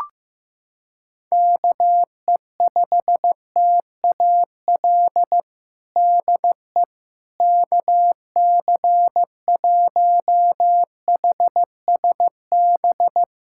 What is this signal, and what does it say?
Morse code, heard as a keyed tone, about 15 wpm, 700 Hz. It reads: KE5TAL DE KC1HSB
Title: USA Callsign DE Callsign - 6 Characters - 15wpm